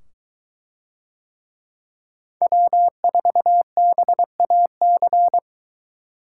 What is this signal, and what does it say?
Morse code, heard as a keyed tone, about 22 wpm, 700 Hz. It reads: W4BAC